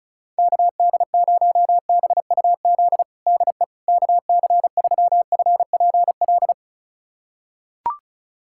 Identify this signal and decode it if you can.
Morse code, heard as a keyed tone, about 36 wpm, 700 Hz. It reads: KD0BUZ DE KC3FPL
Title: USA Callsign DE Callsign - 6 Characters - 35wpm